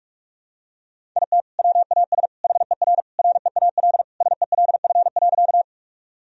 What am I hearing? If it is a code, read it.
Morse code, heard as a keyed tone, about 46 wpm, 700 Hz. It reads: IT WAS HER REAL SELF.
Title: Sentences from Top 1000 Words - Review Part 1 - 45wpm